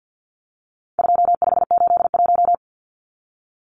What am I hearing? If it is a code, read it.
Morse code, heard as a keyed tone, about 50 wpm, 700 Hz. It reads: 2581
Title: Numbers - 4-Digit 50wpm